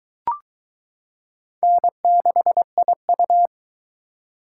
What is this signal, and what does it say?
Morse code, heard as a keyed tone, about 22 wpm, 700 Hz. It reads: N6IU